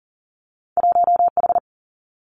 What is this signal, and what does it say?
Morse code, heard as a keyed tone, about 42 wpm, 700 Hz. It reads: 1H
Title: Two Challenging Letter-Number Combinations 40wpm